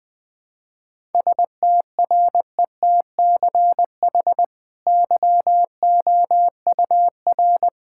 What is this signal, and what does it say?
Morse code, heard as a keyed tone, about 20 wpm, 700 Hz. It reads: STRETCH YOUR
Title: Easy Sentences with Hints - 20wpm